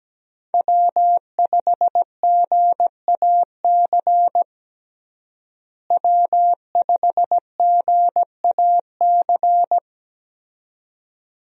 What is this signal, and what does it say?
Morse code, heard as a keyed tone, about 17 wpm, 700 Hz. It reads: W5GAC W5GAC
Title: USA Call Signs - 5 Characters - Repeated 2x - 17wpm